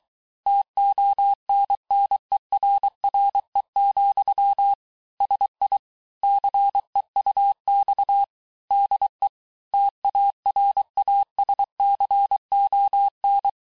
Morse code, heard as a keyed tone, about 23 wpm, 800 Hz. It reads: TONNERRE, SI CEUX DE TARASCON
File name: cours-du-mercredi / audios / COURS-855--texte-clair-vitesse-22-mpm